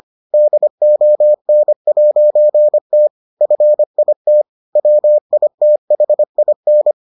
Morse code, heard as a keyed tone, about 25 wpm, 600 Hz. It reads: DON'T FIT WITHIN